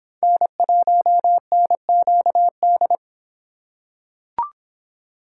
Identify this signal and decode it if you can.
Morse code, heard as a keyed tone, about 25 wpm, 700 Hz. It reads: N1NQD